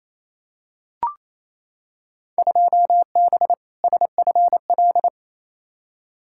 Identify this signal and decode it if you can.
Morse code, heard as a keyed tone, about 28 wpm, 700 Hz. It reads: 2B SFL
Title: Field Day Exchanges Encore - 28wpm